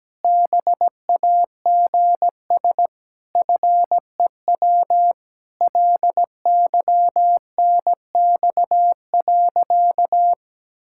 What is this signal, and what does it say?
Morse code, heard as a keyed tone, about 17 wpm, 700 Hz. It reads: BAGS FEW LYNX.
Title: Pangram Challenge v2 - 17wpm